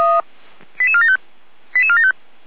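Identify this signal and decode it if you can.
Five-tone selective call (ZVEI), heard as quick runs of five tones, tones of about 70 ms. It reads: 88475 88475
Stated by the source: Five-tone ZVEI selective call received by SDR, logged by the fire brigade (Feuerwehr) Orlamünde.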